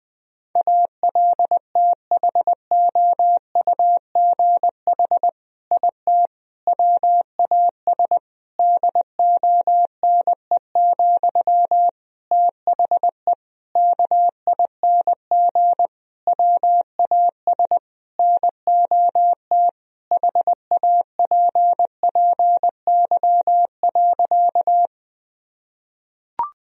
Morse code, heard as a keyed tone, about 20 wpm, 700 Hz. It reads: ALTHOUGH IT WAS DONE, THE KING WAS NOT HAPPY.
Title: Sentences from Top 1000 Words - Review Part 3 - 20wpm